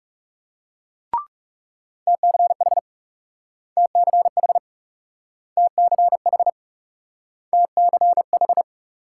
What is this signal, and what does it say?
Morse code, heard as a keyed tone, about 39 wpm, 700 Hz. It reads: TCH TCH TCH TCH